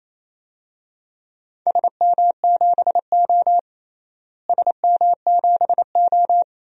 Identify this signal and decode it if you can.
Morse code, heard as a keyed tone, about 28 wpm, 700 Hz. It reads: SM7O SM7O
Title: DXCC-100 Call Signs - 4 Characters - Repeated 2x - 28wpm